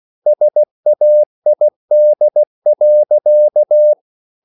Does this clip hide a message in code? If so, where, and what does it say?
Yes: Morse code from 0.3 to 3.9 s, heard as a keyed tone, about 16 wpm, 600 Hz. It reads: SAID.